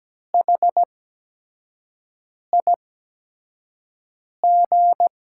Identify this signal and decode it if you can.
Morse code, heard as a keyed tone, about 17 wpm, spaced as for 5 wpm, 700 Hz. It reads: HIG
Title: USA Call Signs - 6 Characters - 8x Character Spacing - 17wpm